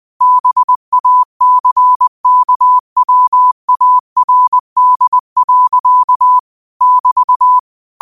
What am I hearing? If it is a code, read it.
Morse code, heard as a keyed tone, about 20 wpm, 1000 Hz. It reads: BACKWARD. <BT>
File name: qotdfiles / 20wpm